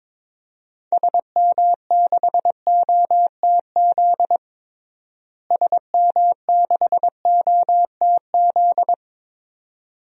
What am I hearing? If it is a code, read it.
Morse code, heard as a keyed tone, about 21 wpm, 700 Hz. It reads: SM6OTZ SM6OTZ
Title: DXCC-100 Call Signs - 6 Characters - Repeated 2x - 22wpm